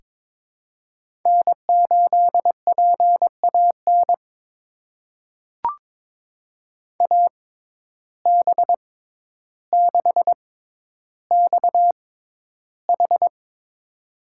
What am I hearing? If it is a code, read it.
Morse code, heard as a keyed tone, about 21 wpm, spaced as for 11 wpm, 700 Hz. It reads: N8PAN AB6XH
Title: USA Call Signs - 5 Characters - 6x Character Spacing - 22wpm